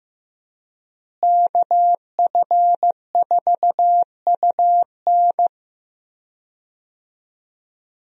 Morse code, heard as a keyed tone, about 15 wpm, 700 Hz. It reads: KF4UN
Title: USA Call Signs - 5 Characters - Repeated 3x - 15wpm